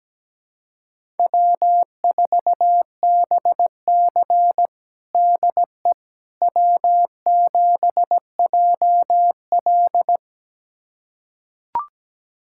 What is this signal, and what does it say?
Morse code, heard as a keyed tone, about 17 wpm, 700 Hz. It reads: W4BC DE W7JL
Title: USA Callsign DE Callsign - 4 Characters - 17wpm